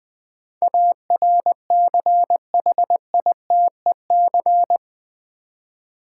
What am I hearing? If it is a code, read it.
Morse code, heard as a keyed tone, about 20 wpm, 700 Hz. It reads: ARCHITEC
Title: Build A Word - 4 Letters At A Time - 12-Letter Words - 20wpm